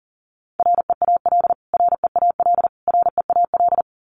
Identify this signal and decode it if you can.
Morse code, heard as a keyed tone, about 40 wpm, 700 Hz. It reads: REAL REAL REAL
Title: Top 300 Words Repeated 40wpm